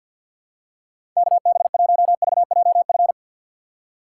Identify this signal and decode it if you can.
Morse code, heard as a keyed tone, about 49 wpm, 700 Hz. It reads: KB1VJF
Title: USA Call Signs - 6 Characters - 4x Character Spacing - 50wpm